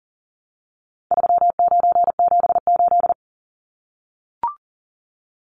Morse code, heard as a keyed tone, about 37 wpm, 700 Hz. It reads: 3978
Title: Numbers - 4-Digit 40wpm